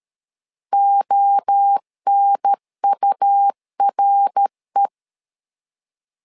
Morse code, heard as a keyed tone, about 13 wpm, 800 Hz. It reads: ONURE
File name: LEZIONI / 3 MWRO / 3 MWRO 12wpm 1730